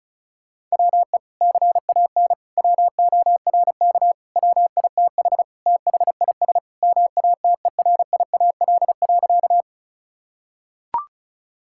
Morse code, heard as a keyed tone, about 34 wpm, 700 Hz. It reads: WE CAN WORK WITH THIS MATERIAL.
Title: Sentences from Top 1000 Words - Review Part 5 - 35wpm